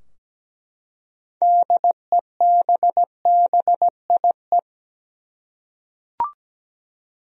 Morse code, heard as a keyed tone, about 17 wpm, 700 Hz. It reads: DEBBIE
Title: USA Common Names - 17wpm